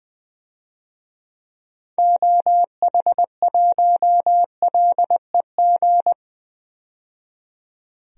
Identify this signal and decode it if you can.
Morse code, heard as a keyed tone, about 21 wpm, 700 Hz. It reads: OH1LEG
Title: DXCC-100 Call Signs - 6 Characters - 20wpm